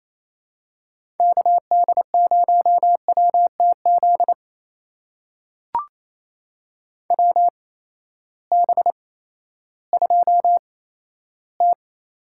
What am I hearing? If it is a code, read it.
Morse code, heard as a keyed tone, about 28 wpm, spaced as for 14 wpm, 700 Hz. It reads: KD0WTZ WB2T
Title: USA Call Signs - 6 Characters - 8x Character Spacing - 28wpm